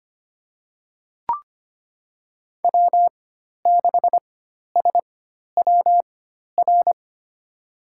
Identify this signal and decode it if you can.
Morse code, heard as a keyed tone, about 25 wpm, spaced as for 12 wpm, 700 Hz. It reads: W6SWR
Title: USA Call Signs - 5 Characters - 4x Character Spacing - 25wpm